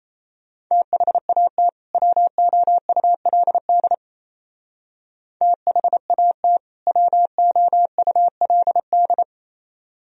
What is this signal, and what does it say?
Morse code, heard as a keyed tone, about 35 wpm, 700 Hz. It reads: THAT WOULD THAT WOULD
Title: Speed-Racing - Sets of 2 Words 28wpm